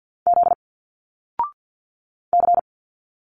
Morse code, heard as a keyed tone, about 50 wpm, 700 Hz. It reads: Z C
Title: Single Letters 50wpm